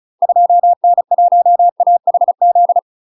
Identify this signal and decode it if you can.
Morse code, heard as a keyed tone, about 35 wpm, 700 Hz. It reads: 2N1AHZ
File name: UK MASTER QSOs TRAINING / 35WPM / UK MASTER QSOs 0002